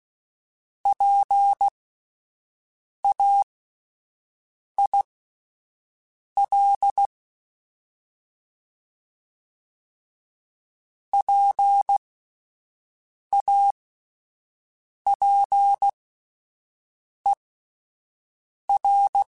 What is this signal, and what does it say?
Morse code, heard as a keyed tone, about 16 wpm, spaced as for 5 wpm, 800 Hz. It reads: PAIL PAPER